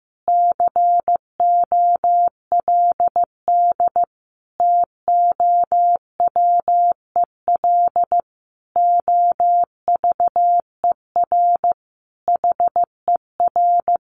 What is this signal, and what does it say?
Morse code, heard as a keyed tone, about 14 wpm, 700 Hz. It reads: COLD TOWEL OVER HER
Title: Easy Phrases 15wpm